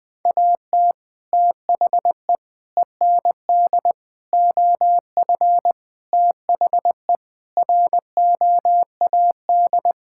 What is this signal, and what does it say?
Morse code, heard as a keyed tone, about 20 wpm, 700 Hz. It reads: AT THE END OF THE ROAD
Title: Sentences with 3-Word Phrases - 20wpm